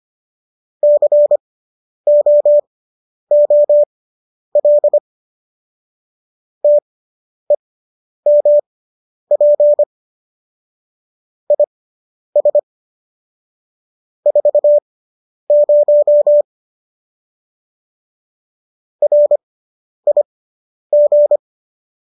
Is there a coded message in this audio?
Yes: Morse code from 0.8 to 21.4 s, heard as a keyed tone, about 25 wpm, spaced as for 10 wpm, 590 Hz. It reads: COOL TEMP IS 40 RIG